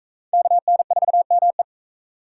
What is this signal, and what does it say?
Morse code, heard as a keyed tone, about 40 wpm, 700 Hz. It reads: KN4ME